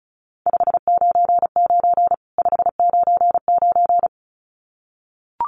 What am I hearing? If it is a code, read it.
Morse code, heard as a keyed tone, about 34 wpm, 700 Hz. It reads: 599 599
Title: Common QSO Phrases 35wpm